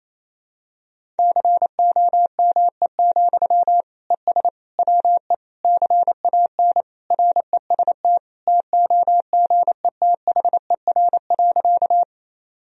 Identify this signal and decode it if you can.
Morse code, heard as a keyed tone, about 28 wpm, 700 Hz. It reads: COME, ES WE CAN REST TOGETHER.
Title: Sentences from Top 500 Words and Abbreviations 28wpm